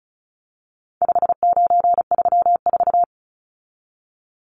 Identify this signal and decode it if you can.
Morse code, heard as a keyed tone, about 34 wpm, 700 Hz. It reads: 5934
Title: Numbers - 4-Digit 35wpm